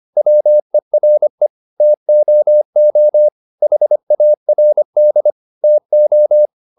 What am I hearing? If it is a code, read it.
Morse code, heard as a keyed tone, about 25 wpm, 600 Hz. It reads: WERE TOO HARD TO